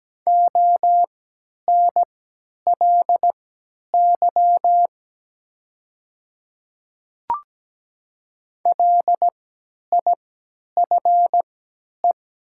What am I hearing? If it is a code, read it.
Morse code, heard as a keyed tone, about 17 wpm, spaced as for 10 wpm, 700 Hz. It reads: ONLY LIFE
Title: 3x Character Spacing - Top 300 Words - 17wpm